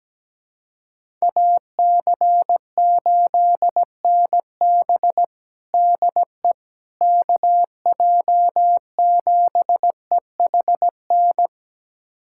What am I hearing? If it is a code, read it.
Morse code, heard as a keyed tone, about 17 wpm, 700 Hz. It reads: AC8NB DE KJ7EHN
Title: USA Callsign DE Callsign 17wpm